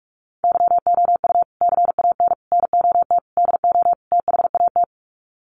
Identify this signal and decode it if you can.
Morse code, heard as a keyed tone, about 44 wpm, 700 Hz. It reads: YOU CAN NOT DO THAT
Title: Sets of 5 Words 45wpm